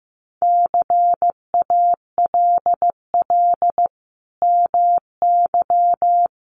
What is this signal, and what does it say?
Morse code, heard as a keyed tone, about 14 wpm, 700 Hz. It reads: CALL MY
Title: Sentences from Top 1500 Words 15wpm